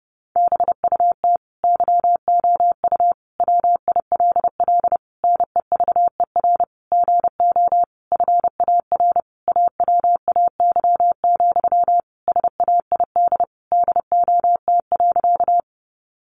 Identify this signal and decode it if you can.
Morse code, heard as a keyed tone, about 30 wpm, 700 Hz. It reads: BUT YOU WILL NEVER GO FAR AWAY, SAID DOT.
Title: Sentences from Top 1500 Words 30wpm